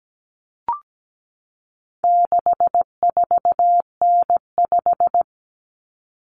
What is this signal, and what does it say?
Morse code, heard as a keyed tone, about 17 wpm, 700 Hz. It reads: 64N5
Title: Common Cut Numbers - 4-Digit 17wpm